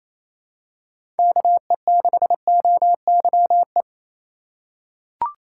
Morse code, heard as a keyed tone, about 28 wpm, 700 Hz. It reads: KE6OYE
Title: USA Call Signs - 2x Character Spacing - 28wpm